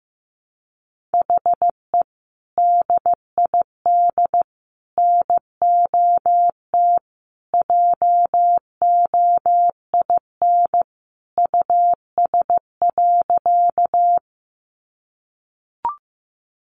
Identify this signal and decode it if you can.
Morse code, heard as a keyed tone, about 15 wpm, 700 Hz. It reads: HE DID NOT JOIN US.
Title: Sentences from Top 1400 Words 15wpm